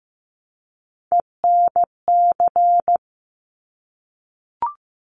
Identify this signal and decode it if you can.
Morse code, heard as a keyed tone, about 15 wpm, 700 Hz. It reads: ENC
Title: Top Three Letter Combinations in Words 15wpm